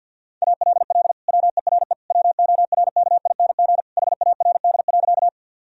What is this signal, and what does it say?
Morse code, heard as a keyed tone, about 51 wpm, 700 Hz. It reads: ALL WERE WORKING HARD.
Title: Sentences from Top 1000 Words - Review Part 4 - 50wpm